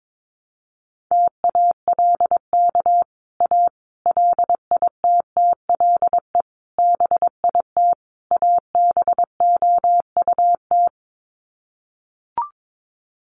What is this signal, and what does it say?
Morse code, heard as a keyed tone, about 22 wpm, 700 Hz. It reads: TALK A LITTLE BIT ABOUT
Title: Sets of 5 Words Encore 22wpm